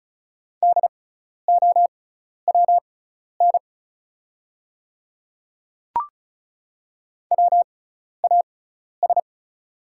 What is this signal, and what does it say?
Morse code, heard as a keyed tone, about 36 wpm, spaced as for 12 wpm, 700 Hz. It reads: DOWN WAS
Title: 6x Character Spacing - Top 200 Words - 35wpm